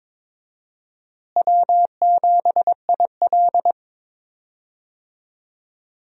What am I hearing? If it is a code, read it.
Morse code, heard as a keyed tone, about 22 wpm, 700 Hz. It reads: W7IL
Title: USA Call Signs - 4 Characters - Repeated 2x - 22wpm